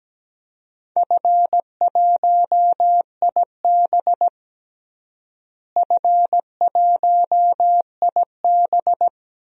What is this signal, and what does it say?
Morse code, heard as a keyed tone, about 17 wpm, 700 Hz. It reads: F1IB F1IB
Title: DXCC-100 Call Signs - 4 Characters - Repeated 3x - 17wpm